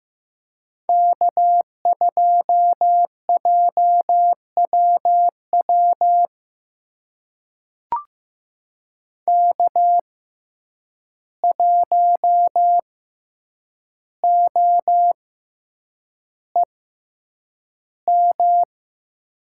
Morse code, heard as a keyed tone, about 15 wpm, spaced as for 8 wpm, 700 Hz. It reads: K2JWW K1OEM
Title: USA Call Signs - 5 Characters - 6x Character Spacing - 15wpm